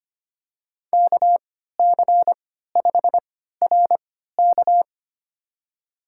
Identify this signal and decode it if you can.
Morse code, heard as a keyed tone, about 25 wpm, spaced as for 14 wpm, 700 Hz. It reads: KC5RK
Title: USA Call Signs - 5 Characters - 3x Character Spacing - 25wpm